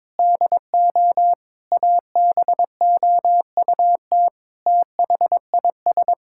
Morse code, heard as a keyed tone, about 22 wpm, 700 Hz. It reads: DO ABOUT THIS
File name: practice / 22wpm / Sentences with 8-Word Phrases - 22wpm